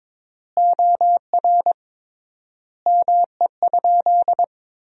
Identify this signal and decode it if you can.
Morse code, heard as a keyed tone, about 22 wpm, 700 Hz. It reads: OR ME?
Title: 3x Word-Spacing - Sentences from Top 200 Words - 22wpm